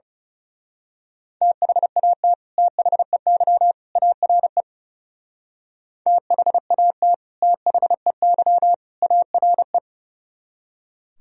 Morse code, heard as a keyed tone, about 30 wpm, 700 Hz. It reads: THAT THEY ARE THAT THEY ARE
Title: Speed-Racing - Sets of 3 Words 30wpm